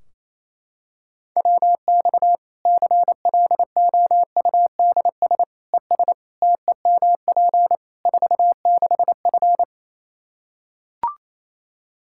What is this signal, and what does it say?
Morse code, heard as a keyed tone, about 28 wpm, 700 Hz. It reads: WX CLOUDS ES TEMP 46F
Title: Practice QSO - Exchange 2 - Info - 28wpm